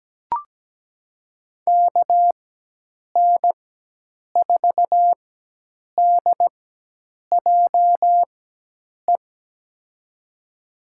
Morse code, heard as a keyed tone, about 17 wpm, spaced as for 8 wpm, 700 Hz. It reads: KN4DJE
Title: USA Call Signs - 6 Characters - 4x Character Spacing - 17wpm